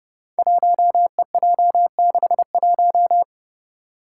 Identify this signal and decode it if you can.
Morse code, heard as a keyed tone, about 30 wpm, 700 Hz. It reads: 1EJ61
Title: Five Challenging Letter-Number Combinations 30wpm